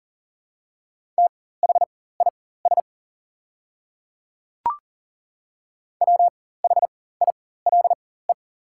Morse code, heard as a keyed tone, about 40 wpm, spaced as for 19 wpm, 700 Hz. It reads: THIS WHILE
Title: 4x Character Spacing - Top 300 Words - 40wpm